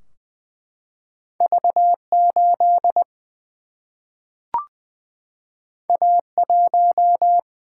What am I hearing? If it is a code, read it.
Morse code, heard as a keyed tone, about 20 wpm, 700 Hz. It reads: V 8 A 1
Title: Single-Letter Single-Number 20wpm